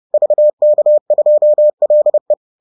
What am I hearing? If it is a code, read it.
Morse code, heard as a keyed tone, about 30 wpm, 600 Hz. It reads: VK2LE